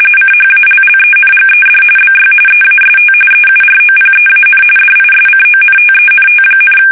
RTTY, about 45.45 baud. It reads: CHALLENGED BY EAST EUROPEAN DISCONTENT,
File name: fsk850 / battleshipnj